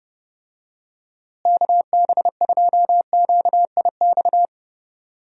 Morse code, heard as a keyed tone, about 30 wpm, 700 Hz. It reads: KB2QIX
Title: USA Call Signs - 6 Characters - Repeated 3x - 30wpm